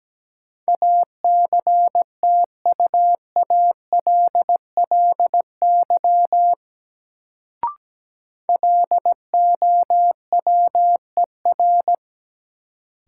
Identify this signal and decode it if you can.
Morse code, heard as a keyed tone, about 17 wpm, 700 Hz. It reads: ACTUALLY LOWER
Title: Sentences from Top 700 Words - New Word Drill - 17wpm